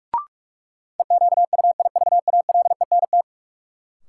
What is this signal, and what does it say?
Morse code, heard as a keyed tone, about 46 wpm, 700 Hz. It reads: EQUIVALENT
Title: Sentences from Top 1600 Words - New Word Drill Alt - 45wpm